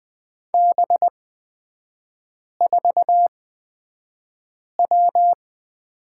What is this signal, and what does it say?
Morse code, heard as a keyed tone, about 20 wpm, spaced as for 5 wpm, 700 Hz. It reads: B4W
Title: 255 Lesson 41e - Period - Three Characters 20wpm